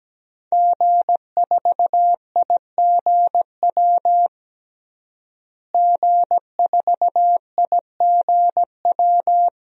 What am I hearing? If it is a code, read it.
Morse code, heard as a keyed tone, about 17 wpm, 700 Hz. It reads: G4IGW G4IGW
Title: DXCC-100 Call Signs - 5 Characters - Repeated 2x - 17wpm